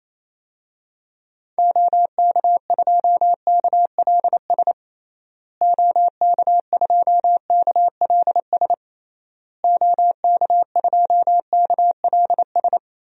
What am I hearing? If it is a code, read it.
Morse code, heard as a keyed tone, about 28 wpm, 700 Hz. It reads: OK2KLS OK2KLS OK2KLS
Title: DXCC-100 Call Signs - 6 Characters - Repeated 3x - 28wpm